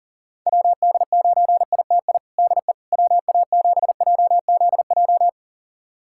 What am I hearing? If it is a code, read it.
Morse code, heard as a keyed tone, about 37 wpm, 700 Hz. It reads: WD9ITI DE WA7JZJ
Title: USA Callsign DE Callsign - 6 Characters - 40wpm